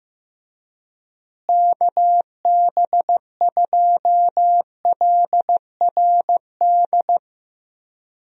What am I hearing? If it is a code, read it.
Morse code, heard as a keyed tone, about 15 wpm, 700 Hz. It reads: KB2LRD
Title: USA Call Signs - 6 Characters - Repeated 3x - 15wpm